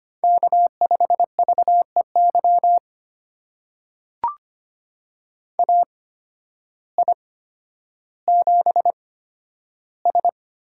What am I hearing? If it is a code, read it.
Morse code, heard as a keyed tone, about 25 wpm, spaced as for 12 wpm, 700 Hz. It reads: K5VEY AI7S
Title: USA Call Signs - 8x Character Spacing - 25wpm